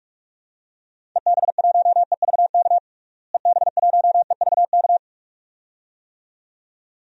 Morse code, heard as a keyed tone, about 46 wpm, 690 Hz. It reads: EB1EVK EB1EVK
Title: DXCC-100 Call Signs - 6 Characters - Repeated 2x - 45wpm